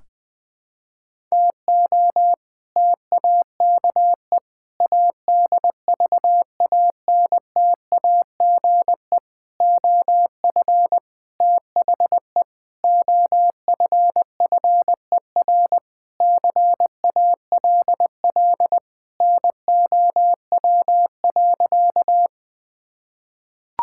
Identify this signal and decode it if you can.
Morse code, heard as a keyed tone, about 20 wpm, 700 Hz. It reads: TO TAKE ADVANTAGE OF THE OFFER CALL NOW.
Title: Sentences with 4-Word Phrases - Spoken Beginning - 20wpm